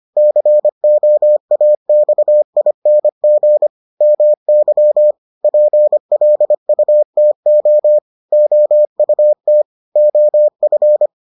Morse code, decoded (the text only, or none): COAXING MY PLUTO OUT OF